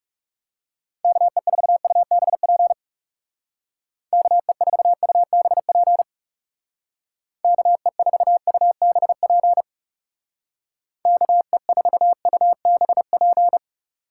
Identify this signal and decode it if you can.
Morse code, heard as a keyed tone, about 37 wpm, 700 Hz. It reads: KE4UBP KE4UBP KE4UBP KE4UBP
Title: Speed Racing - USA 2x3 Call Signs 30wpm